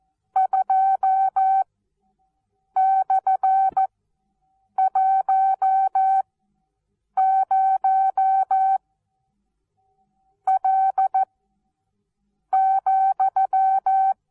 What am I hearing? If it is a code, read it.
Morse code, heard as a keyed tone, about 14 wpm, spaced as for 6 wpm, 770 Hz. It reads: 2/10L,
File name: Onlinelehrgang / morsen / DJ4UF Morsen19